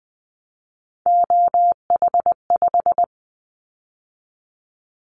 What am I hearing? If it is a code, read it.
Morse code, heard as a keyed tone, about 19 wpm, 700 Hz. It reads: OH5
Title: Callsign Prefixes for Top 25 DXCC - 20wpm